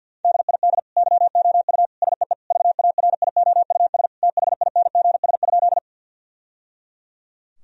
Morse code, heard as a keyed tone, about 50 wpm, 700 Hz. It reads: DID YOU SEE VARIOUS THINGS?
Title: Sentences from Top 1000 Words - Review Part 4 - 50wpm